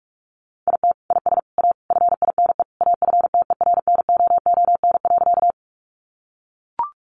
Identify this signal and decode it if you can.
Morse code, heard as a keyed tone, about 45 wpm, 700 Hz. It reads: IT IS A FINE AFTERNOON.
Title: Sentences from Top 1500 Words 45wpm